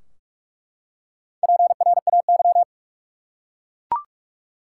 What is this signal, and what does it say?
Morse code, heard as a keyed tone, about 41 wpm, 700 Hz. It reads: PRAY
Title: Four Letter Words - 45wpm